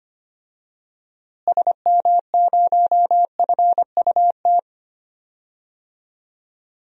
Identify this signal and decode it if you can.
Morse code, heard as a keyed tone, about 25 wpm, 700 Hz. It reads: SM0FUT
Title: DXCC-100 Call Signs - 6 Characters - Repeated 2x - 25wpm